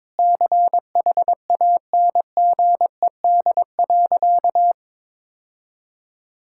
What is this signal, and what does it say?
Morse code, heard as a keyed tone, about 22 wpm, 700 Hz. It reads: CHANGED.
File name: practice / 22wpm / Sentences with 4-Word Phrases - 22wpm